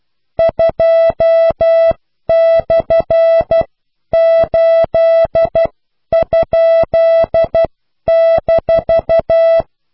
Morse code, heard as a keyed tone, about 12 wpm, spaced as for 10 wpm, 660 Hz. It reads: 2/8?-